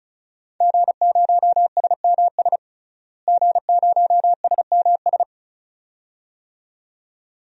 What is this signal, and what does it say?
Morse code, heard as a keyed tone, about 34 wpm, 700 Hz. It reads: G0SMS G0SMS
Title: DXCC-100 Call Signs - 5 Characters - Repeated 2x - 35wpm